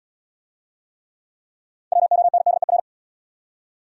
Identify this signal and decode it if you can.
Morse code, heard as a keyed tone, about 68 wpm, 700 Hz. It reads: WONDER